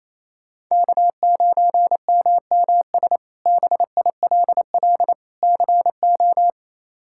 Morse code, heard as a keyed tone, about 28 wpm, 700 Hz. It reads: K9MMS BILL CO
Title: K1USN SST QSO Practice - 28wpm